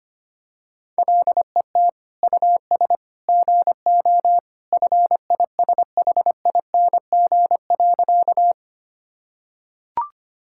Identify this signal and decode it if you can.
Morse code, heard as a keyed tone, about 25 wpm, 700 Hz. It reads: LET US GO FISHING.
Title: Sentences from Top 1000 Words - Review Part 5 - 25wpm